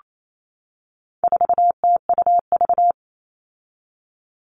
Morse code, heard as a keyed tone, about 28 wpm, 700 Hz. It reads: 4TUV
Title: All Cut Numbers - 4-Digit 28wpm